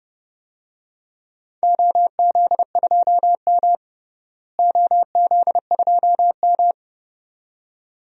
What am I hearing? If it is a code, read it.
Morse code, heard as a keyed tone, about 31 wpm, 700 Hz. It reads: OZ2M OZ2M
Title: DXCC-100 Call Signs - 4 Characters - Repeated 2x - 30wpm